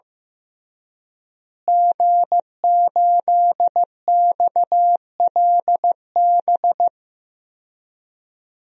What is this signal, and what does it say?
Morse code, heard as a keyed tone, about 15 wpm, 700 Hz. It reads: G8XLB
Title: DXCC-100 Call Signs - 5 Characters - Repeated 3x - 15wpm